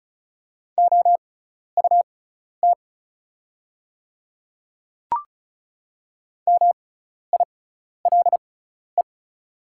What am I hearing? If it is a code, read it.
Morse code, heard as a keyed tone, about 36 wpm, spaced as for 12 wpm, 690 Hz. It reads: OUT MILE
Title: 6x Character Spacing - Top 300 Words - 35wpm